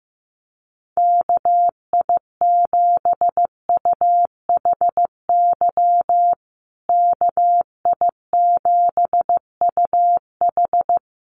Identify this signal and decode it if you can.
Morse code, heard as a keyed tone, about 15 wpm, 700 Hz. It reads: KI7UHY KI7UH
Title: USA Call Signs Repeated 2x 15wpm